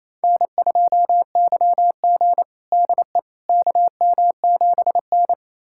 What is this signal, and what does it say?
Morse code, heard as a keyed tone, about 27 wpm, 700 Hz. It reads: N2YG DE KM7N
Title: USA Callsign DE Callsign - 4 Characters - 28wpm